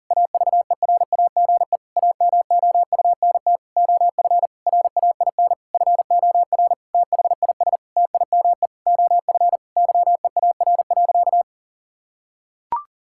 Morse code, heard as a keyed tone, about 40 wpm, 700 Hz. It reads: AVERAGE AMOUNT OF RAIN FOR THIS TIME OF YEAR.
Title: Sentences from Top 1000 Words - Review Part 6 - 40wpm